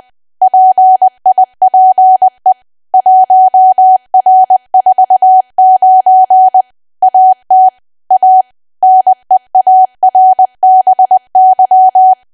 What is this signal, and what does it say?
Morse code, heard as a keyed tone, about 20 wpm, 750 Hz. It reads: PIPE 1R49 AT A NEARBY
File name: cwa / W1AW-Level3 / 20WPM